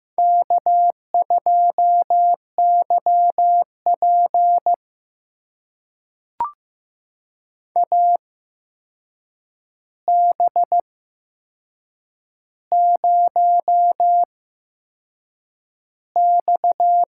Morse code, heard as a keyed tone, about 15 wpm, spaced as for 6 wpm, 700 Hz. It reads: K2YP AB0X
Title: USA Call Signs - 8x Character Spacing - 15wpm